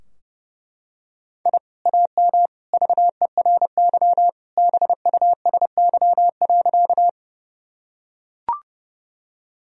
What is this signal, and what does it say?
Morse code, heard as a keyed tone, about 30 wpm, 700 Hz. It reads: I AM VERY BUSY.
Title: Sentences from Top 1000 Words - Review Part 3 - 30wpm